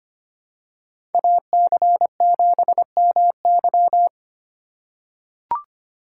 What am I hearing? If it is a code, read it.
Morse code, heard as a keyed tone, about 25 wpm, 700 Hz. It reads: AC7MY
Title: USA Call Signs - 2x Character Spacing - 25wpm